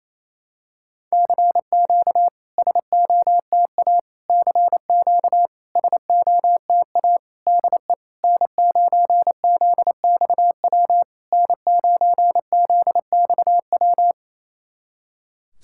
Morse code, heard as a keyed tone, about 28 wpm, 700 Hz. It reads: CQ SOTA CQ SOTA DE N9ZXW N9ZXW
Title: SOTA QSO Practice 28wpm